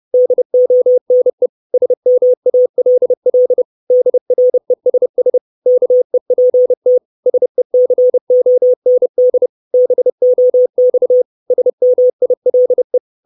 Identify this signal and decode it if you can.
Morse code, heard as a keyed tone, about 30 wpm, 500 Hz. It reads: DONE SMALL DRESS KEPT SECOND BOX SMILE
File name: 632 Words 30WPM